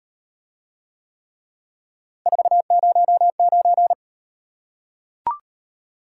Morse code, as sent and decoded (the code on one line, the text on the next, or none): ....- ----- ----.
409